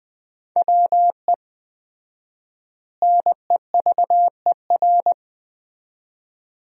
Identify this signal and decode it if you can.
Morse code, heard as a keyed tone, about 20 wpm, 700 Hz. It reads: WE NEVER
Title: 4x Word-Spacing - Sentences from Top 300 Words - 20wpm